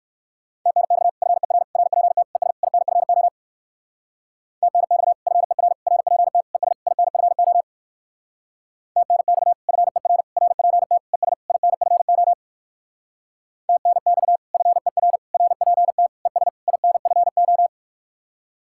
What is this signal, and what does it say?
Morse code, heard as a keyed tone, about 64 wpm, 700 Hz. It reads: TNX FER RPT ES INFO TNX FER RPT ES INFO TNX FER RPT ES INFO TNX FER RPT ES INFO